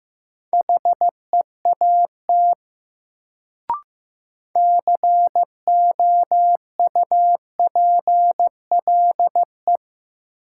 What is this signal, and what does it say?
Morse code, heard as a keyed tone, about 15 wpm, 700 Hz. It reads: HEAT COUPLE
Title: Sentences from Top 1100 Words - New Word Drill Alt - 15wpm